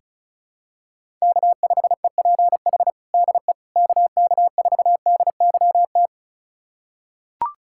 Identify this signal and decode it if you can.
Morse code, heard as a keyed tone, about 36 wpm, 700 Hz. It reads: K5EPH DE KK4DYT
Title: USA Callsign DE Callsign 35wpm